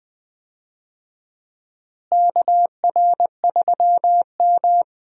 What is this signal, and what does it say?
Morse code, heard as a keyed tone, about 21 wpm, 700 Hz. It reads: KR3M
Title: USA Call Signs - 4 Characters - Repeated 3x - 20wpm